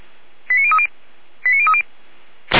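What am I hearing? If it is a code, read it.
Five-tone selective call (ZVEI), heard as quick runs of five tones, tones of about 70 ms. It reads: 89030 89030
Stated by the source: ZVEI five-tone selective call received on SDR.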